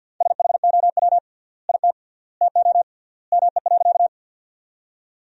Morse code, heard as a keyed tone, about 52 wpm, 700 Hz. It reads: SHOW IT TO ME.
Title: 3x Word-Spacing - Sentences from Top 200 Words - 50wpm